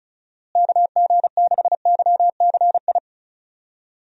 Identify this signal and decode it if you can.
Morse code, heard as a keyed tone, about 34 wpm, 700 Hz. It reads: KG6YCI
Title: USA Call Signs - 6 Characters - Repeated 3x - 35wpm